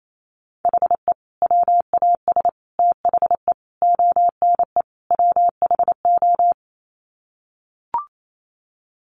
Morse code, heard as a keyed tone, about 28 wpm, 700 Hz. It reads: HE WAS THE ONE WHO
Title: Sets of 5 Words Encore2 28wpm